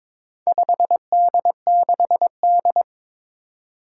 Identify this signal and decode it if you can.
Morse code, heard as a keyed tone, about 22 wpm, 700 Hz. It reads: 5D6D
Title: Four Challenging Letter-Number Combinations 22wpm